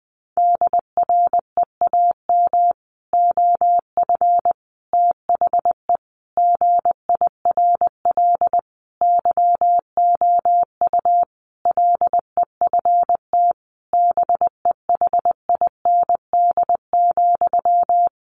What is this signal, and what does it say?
Morse code, heard as a keyed tone, about 20 wpm, 700 Hz. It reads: DREAM OF THE GIRL YOU LEFT BEHIND,
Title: Sentences from Top 1500 Words 20wpm